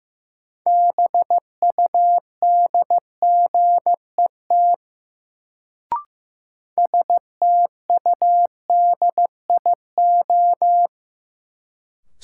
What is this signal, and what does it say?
Morse code, heard as a keyed tone, about 15 wpm, 700 Hz. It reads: BUDGET STUDIO
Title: Sentences from Top 1200 Words - New Word Drill Alt - 15wpm